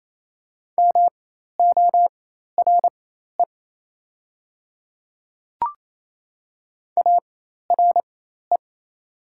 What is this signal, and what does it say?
Morse code, heard as a keyed tone, about 28 wpm, spaced as for 13 wpm, 700 Hz. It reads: MORE ARE
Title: 4x Character Spacing - Top 100 Words - 28wpm